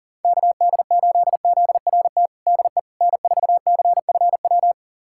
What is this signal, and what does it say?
Morse code, heard as a keyed tone, about 37 wpm, 700 Hz. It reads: KD8ZRT DE N4CFW
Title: USA Callsign DE Callsign 40wpm